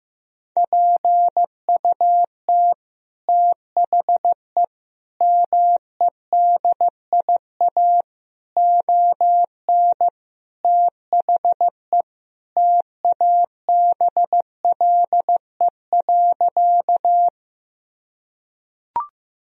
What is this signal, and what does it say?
Morse code, heard as a keyed tone, about 15 wpm, 700 Hz. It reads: PUT THE MEDIA ON THE TABLE.
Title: Sentences from Top 1000 Words - Review Part 4 - 15wpm